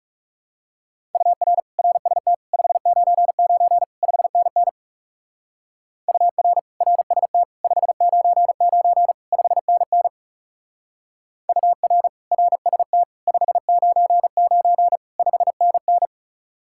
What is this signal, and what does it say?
Morse code, heard as a keyed tone, about 37 wpm, 700 Hz. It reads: UR RST 599 5NN UR RST 599 5NN UR RST 599 5NN
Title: Speed-Racing Common QSO Phrases Encore - 30wpm